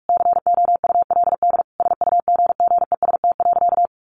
Morse code, heard as a keyed tone, about 44 wpm, 700 Hz. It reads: COULD SUGGEST.
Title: Sentences from Top 1500 Words 45wpm